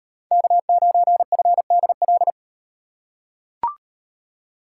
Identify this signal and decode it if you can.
Morse code, heard as a keyed tone, about 38 wpm, 700 Hz. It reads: K9FDL